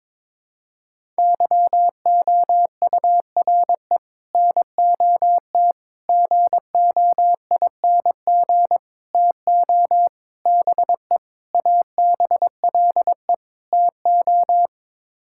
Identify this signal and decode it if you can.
Morse code, heard as a keyed tone, about 22 wpm, 700 Hz. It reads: YOURE NOT GOING TO BE ABLE TO
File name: practice / 22wpm / Sets of 7 Words - 22wpm